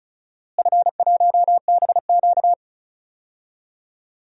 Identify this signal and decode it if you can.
Morse code, heard as a keyed tone, about 36 wpm, 700 Hz. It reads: F1BQ
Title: DXCC-100 Call Signs - 4 Characters - Repeated 2x - 35wpm